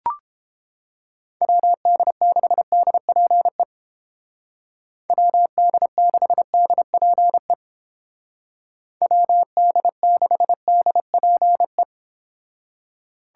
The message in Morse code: WD6DPE WD6DPE WD6DPE

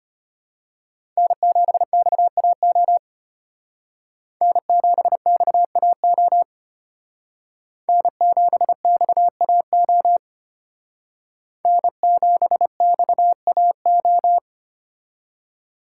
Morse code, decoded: N7XAO N7XAO N7XAO N7XAO

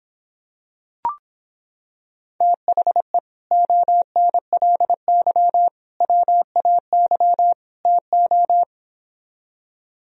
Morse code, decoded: THE ONLY WAY TO

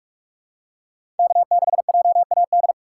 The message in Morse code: K6JAD

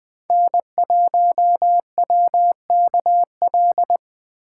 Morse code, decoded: N1WKL

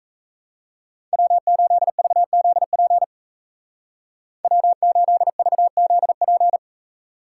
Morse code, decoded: W8VZP W8VZP

